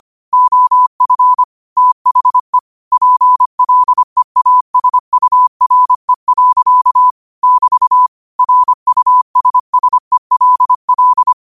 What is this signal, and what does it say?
Morse code, heard as a keyed tone, about 25 wpm, 1000 Hz. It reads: OF THE PLEASURE. <BT> RUSSELL
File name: qotdfiles / 25wpm